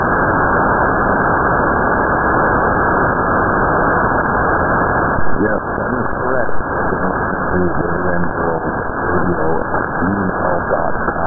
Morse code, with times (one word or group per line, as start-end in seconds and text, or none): none